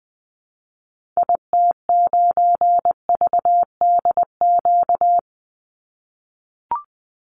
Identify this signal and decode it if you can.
Morse code, heard as a keyed tone, about 20 wpm, 700 Hz. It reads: IT9VDQ
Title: Call Signs - 20wpm